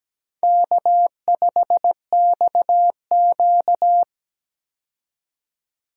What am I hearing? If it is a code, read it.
Morse code, heard as a keyed tone, about 17 wpm, 700 Hz. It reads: K5XQ
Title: K1USN SST QSO Practice - 17wpm